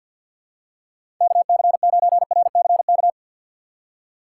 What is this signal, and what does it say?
Morse code, heard as a keyed tone, about 51 wpm, 700 Hz. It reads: KC9RCK
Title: USA Call Signs - 6 Characters - 4x Character Spacing - 50wpm